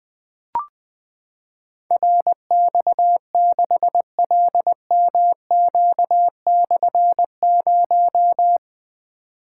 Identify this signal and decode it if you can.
Morse code, heard as a keyed tone, about 21 wpm, 700 Hz. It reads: RX6LMQ/0
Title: Difficult Call Signs 20wpm